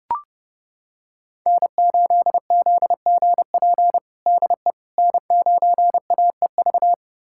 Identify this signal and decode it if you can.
Morse code, heard as a keyed tone, about 32 wpm, 700 Hz. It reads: N8ZGP DE N9AEV
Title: USA Callsign DE Callsign - 5 Characters - 30wpm